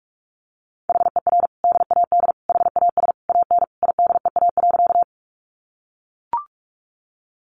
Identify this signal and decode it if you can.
Morse code, heard as a keyed tone, about 44 wpm, 700 Hz. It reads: HER DAD HAS AN IDEA.
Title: Additional Easy Phrases 45wpm